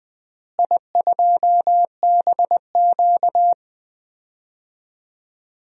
Morse code, heard as a keyed tone, about 21 wpm, 700 Hz. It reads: I2BQ
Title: DXCC-100 Call Signs - 4 Characters - Repeated 3x - 20wpm